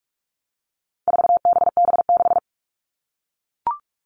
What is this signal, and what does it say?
Morse code, heard as a keyed tone, about 44 wpm, 700 Hz. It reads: 4BB6
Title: All Cut Numbers - 4-Digit 45wpm